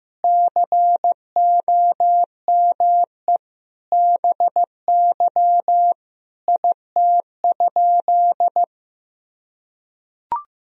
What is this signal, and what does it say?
Morse code, heard as a keyed tone, about 16 wpm, 700 Hz. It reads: COME BY IT?
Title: Sentences from Top 100 Words and Abbreviations 15wpm